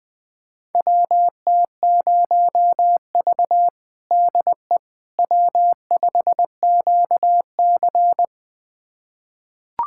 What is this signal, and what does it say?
Morse code, heard as a keyed tone, about 21 wpm, 700 Hz. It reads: WT0V DE W5QC
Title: USA Callsign DE Callsign - 4 Characters - 20wpm